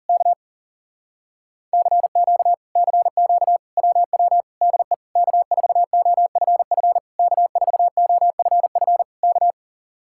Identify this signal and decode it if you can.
Morse code, heard as a keyed tone, about 40 wpm, 700 Hz. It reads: K CQ CQ WW DE K4OFF K4OFF K